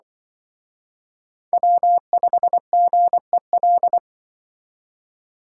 Morse code, heard as a keyed tone, about 25 wpm, 700 Hz. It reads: W5GEL
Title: Speed Racing - USA 1x3 Call Signs 20wpm